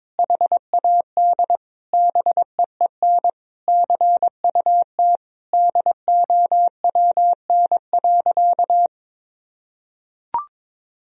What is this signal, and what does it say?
Morse code, heard as a keyed tone, about 22 wpm, 700 Hz. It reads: HAD BEEN CUT DOWN.
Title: Sentences from Top 1000 Words - Review Part 1 - 22wpm